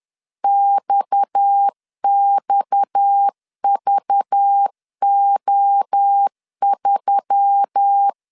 Morse code, heard as a keyed tone, about 10 wpm, 800 Hz. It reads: XXVO3
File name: CW / DET / 1 6 1500C S B 10wpm 4137